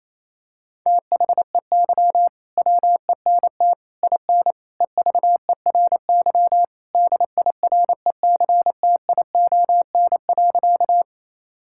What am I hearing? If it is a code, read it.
Morse code, heard as a keyed tone, about 28 wpm, 700 Hz. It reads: THEY WENT IN EVERY DIRECTION.
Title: Sentences from Top 1000 Words - Review Part 5 - 28wpm